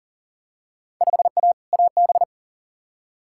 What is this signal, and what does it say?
Morse code, heard as a keyed tone, about 40 wpm, 700 Hz. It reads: 5A AB
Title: Field Day Exchanges Encore - 40wpm